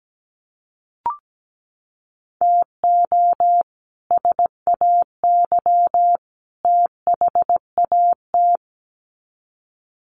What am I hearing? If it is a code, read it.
Morse code, heard as a keyed tone, about 17 wpm, 700 Hz. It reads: TO SAY THAT
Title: Sets of 3 Words 17wpm